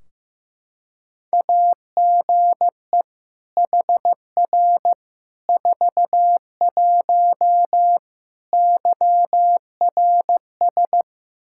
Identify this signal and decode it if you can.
Morse code, heard as a keyed tone, about 16 wpm, 700 Hz. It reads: AGE HR 41 YRS
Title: Practice QSO - Exchange 3 - More Info - 15wpm